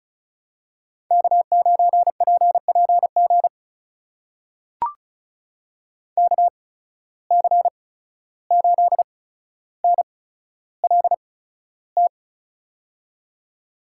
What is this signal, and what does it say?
Morse code, heard as a keyed tone, about 34 wpm, spaced as for 14 wpm, 700 Hz. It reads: K9PPG KC8NLT
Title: USA Call Signs - 8x Character Spacing - 35wpm